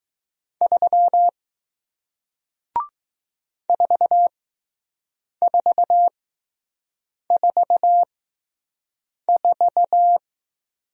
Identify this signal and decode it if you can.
Morse code, heard as a keyed tone, about 20 wpm, spaced as for 6 wpm, 700 Hz. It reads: 3 4444